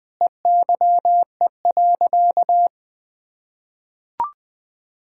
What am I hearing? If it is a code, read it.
Morse code, heard as a keyed tone, about 20 wpm, 700 Hz. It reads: EYE.
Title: Sentences with 6-Word Phrases - 20wpm